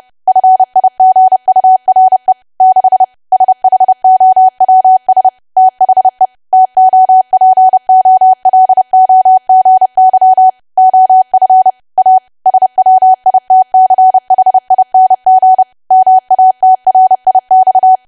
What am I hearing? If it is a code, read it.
Morse code, heard as a keyed tone, about 30 wpm, 750 Hz. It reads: FIGURE 6 SHOWS THE TOPOLOGY OF A SWITCHING MATRIX
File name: cwa / W1AW-Level3 / 30WPM